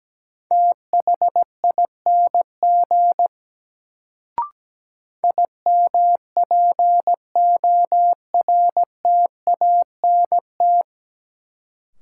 Morse code, heard as a keyed tone, about 17 wpm, 700 Hz. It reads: THING IMPORTANT
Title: Sets of 3 Words - New Word Drill - 17wpm